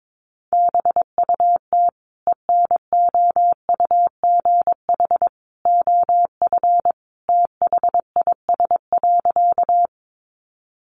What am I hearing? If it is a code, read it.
Morse code, heard as a keyed tone, about 22 wpm, 700 Hz. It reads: BUT ENOUGH OF THIS.
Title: Sentences from Top 400 Words 22wpm